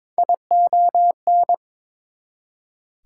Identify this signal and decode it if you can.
Morse code, heard as a keyed tone, about 21 wpm, 700 Hz. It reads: ION
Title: Prefixes and Suffixes 22wpm